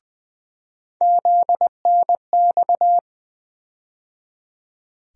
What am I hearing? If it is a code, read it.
Morse code, heard as a keyed tone, about 20 wpm, 700 Hz. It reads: ZNX
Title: USA Call Sign Postfixes - 3 Letters 20wpm